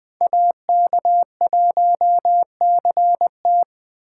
Morse code, heard as a keyed tone, about 25 wpm, 700 Hz. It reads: AK1CT